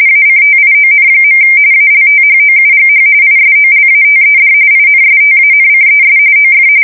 RTTY, about 45.45 baud. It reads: ACTION. ALTHOUGH ISOLATED INSTANCES OF
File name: fsk170 / usnavy2